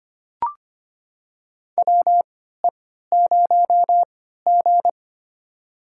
Morse code, heard as a keyed tone, about 25 wpm, spaced as for 14 wpm, 700 Hz. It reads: WE0G
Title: USA Call Signs - 4 Characters - 3x Character Spacing - 25wpm